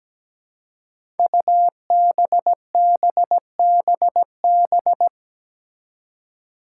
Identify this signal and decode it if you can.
Morse code, heard as a keyed tone, about 17 wpm, 700 Hz. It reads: UBBBB
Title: Five Challenging Letter-Number Combinations 17wpm